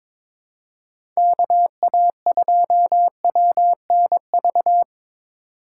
Morse code, heard as a keyed tone, about 22 wpm, 700 Hz. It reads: KA2WNV
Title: USA Call Signs - 6 Characters - 3x Character Spacing - 22wpm